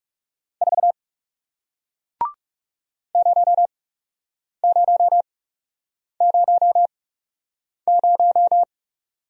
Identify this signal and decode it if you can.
Morse code, heard as a keyed tone, about 40 wpm, spaced as for 8 wpm, 700 Hz. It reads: 4 0000